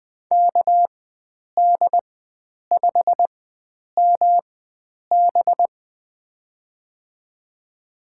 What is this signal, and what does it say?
Morse code, heard as a keyed tone, about 20 wpm, spaced as for 9 wpm, 700 Hz. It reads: KD5MB
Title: USA Call Signs - 5 Characters - 4x Character Spacing - 20wpm